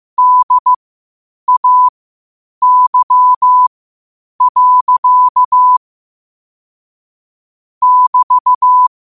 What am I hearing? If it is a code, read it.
Morse code, heard as a keyed tone, about 15 wpm, spaced as for 8 wpm, 1000 Hz. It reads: DAY. <BT>
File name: qotdfiles / 8wpm